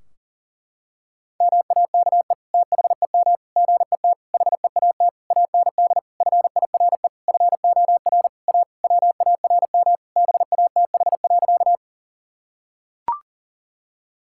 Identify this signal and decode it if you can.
Morse code, heard as a keyed tone, about 37 wpm, 700 Hz. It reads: MAKE THEM GET HEAT AND FIRE FOR A WARM BATH.
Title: Sentences from Top 1600 Words 40wpm